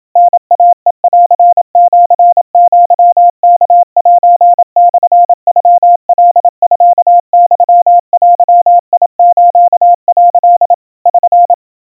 Morse code (Japanese Wabun code), heard as a keyed tone, about 27 wpm, 700 Hz. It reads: タイヘンシアワセモノカミユデス」 <SN>
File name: kawa / Morse sample / kamyu 100 cpm 700 Hz Crispy